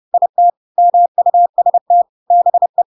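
Morse code, heard as a keyed tone, about 30 wpm, 700 Hz. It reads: IT MUST BE
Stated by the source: Chapter 2 of The Adventures of Pinocchio read as a book in Morse code (CW) at 30 wpm, by ik1hge.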